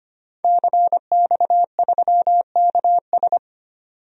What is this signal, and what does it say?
Morse code, heard as a keyed tone, about 25 wpm, 700 Hz. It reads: CX3KS
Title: DXCC-100 Call Signs - 5 Characters - 25wpm